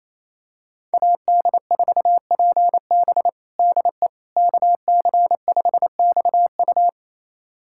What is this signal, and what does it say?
Morse code, heard as a keyed tone, about 28 wpm, 700 Hz. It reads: AD4PB DE KC5XU
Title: USA Callsign DE Callsign - 5 Characters - 28wpm